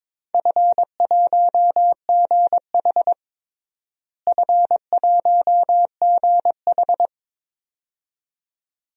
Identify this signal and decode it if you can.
Morse code, heard as a keyed tone, about 22 wpm, 700 Hz. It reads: F1GH F1GH
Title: DXCC-100 Call Signs - 4 Characters - Repeated 2x - 22wpm